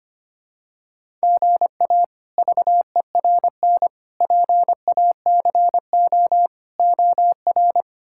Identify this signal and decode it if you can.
Morse code, heard as a keyed tone, about 25 wpm, 700 Hz. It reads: GA VERN PACO OR
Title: K1USN SST QSO Practice - 25wpm